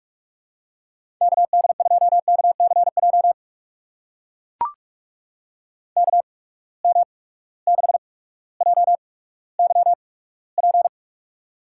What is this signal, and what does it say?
Morse code, heard as a keyed tone, about 42 wpm, spaced as for 19 wpm, 700 Hz. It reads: KD2KCJ KM6JYP
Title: USA Call Signs - 6 Characters - 8x Character Spacing - 45wpm